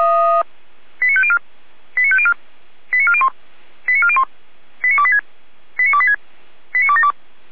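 Five-tone selective call (ZVEI), heard as quick runs of five tones, tones of about 70 ms. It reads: 89504 89504 89401 89401 89286 89286 89272
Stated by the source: Received on an SDR.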